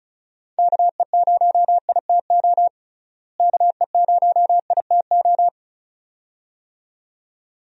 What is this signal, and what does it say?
Morse code, heard as a keyed tone, about 36 wpm, 700 Hz. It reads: KE0ITO KE0ITO
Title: USA Call Signs - 6 Characters - Repeated 2x - 35wpm